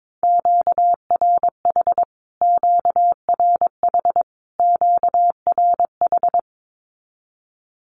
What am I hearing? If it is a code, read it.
Morse code, heard as a keyed tone, about 22 wpm, 700 Hz. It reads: QRH QRH QRH
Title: Q-Codes Repeated 3x 22wpm